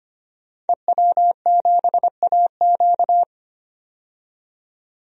Morse code, heard as a keyed tone, about 25 wpm, 700 Hz. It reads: EW7AQ
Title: DXCC-100 Call Signs - 5 Characters - 25wpm